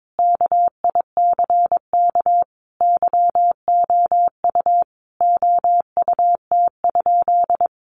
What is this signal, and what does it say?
Morse code, heard as a keyed tone, about 22 wpm, 700 Hz. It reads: KICK YOU OUT?
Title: Sentences from Top 1300 Words 22wpm